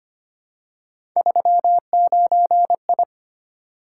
Morse code, heard as a keyed tone, about 25 wpm, 700 Hz. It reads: 39I
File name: Three Letter-Number Combinations - 25wpm